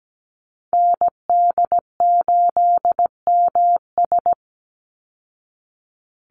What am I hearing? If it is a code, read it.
Morse code, heard as a keyed tone, about 17 wpm, 700 Hz. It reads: ND8MS
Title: Call Signs Repeated 3x - 17wpm